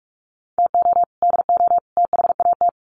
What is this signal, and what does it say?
Morse code, heard as a keyed tone, about 44 wpm, 700 Hz. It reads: TO DO THAT
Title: Sets of 3 Words 45wpm